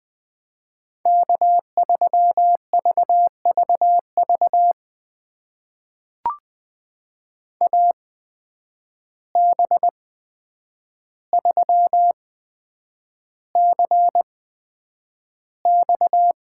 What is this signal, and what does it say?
Morse code, heard as a keyed tone, about 20 wpm, spaced as for 9 wpm, 700 Hz. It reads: K3VVV AB3CX
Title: USA Call Signs - 5 Characters - 8x Character Spacing - 20wpm